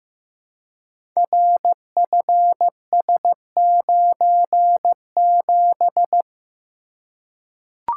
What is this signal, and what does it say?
Morse code, heard as a keyed tone, about 15 wpm, 700 Hz. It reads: RFS97
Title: Code Groups - 5 Letter:Numbers - 15wpm